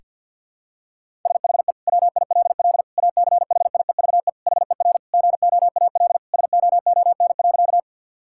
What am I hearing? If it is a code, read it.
Morse code, heard as a keyed tone, about 50 wpm, 700 Hz. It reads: SHE WILL ACHIEVE HER GOAL SOON.
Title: Sentences from Top 1000 Words - Review Part 6 - 50wpm